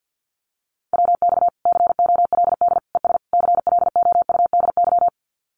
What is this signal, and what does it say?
Morse code, heard as a keyed tone, about 50 wpm, 700 Hz. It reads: WX COLD ES CLOUDY
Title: Common QSO Phrases 50wpm